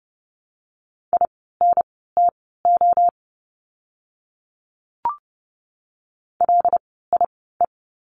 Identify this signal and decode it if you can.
Morse code, heard as a keyed tone, about 30 wpm, spaced as for 17 wpm, 700 Hz. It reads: INTO LIE
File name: Lesson 16f - Letter c - Words 30wpm